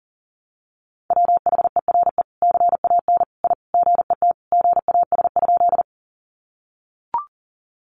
Morse code, heard as a keyed tone, about 40 wpm, 700 Hz. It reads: WHERE CAN I GET GAS?
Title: Sentences from Top 1100 Words 40wpm